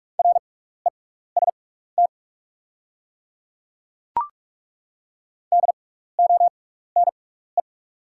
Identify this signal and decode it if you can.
Morse code, heard as a keyed tone, about 42 wpm, spaced as for 15 wpm, 690 Hz. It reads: REST DONE
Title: 6x Character Spacing - Top 500 Words - 45wpm